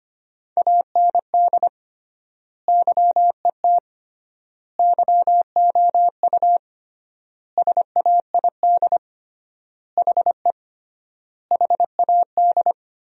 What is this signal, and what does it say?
Morse code, heard as a keyed tone, about 25 wpm, 700 Hz. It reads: AND YET YOU SAID HE HAD
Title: 3x Word-Spacing - Sentences from Top 500 Words - 25wpm